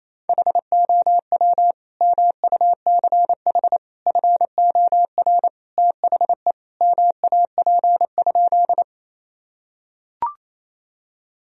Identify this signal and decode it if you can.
Morse code, heard as a keyed tone, about 27 wpm, 700 Hz. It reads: HOW MUCH FOR THE MAP?
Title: Sentences from Top 1000 Words - Review Part 2 - 28wpm